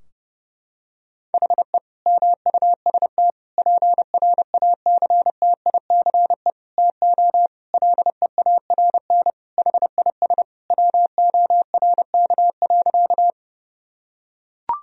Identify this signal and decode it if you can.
Morse code, heard as a keyed tone, about 30 wpm, 700 Hz. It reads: HE MUST PRACTICE TO LEARN HIS WORK.
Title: Sentences from Top 1000 Words - Review Part 4 - 30wpm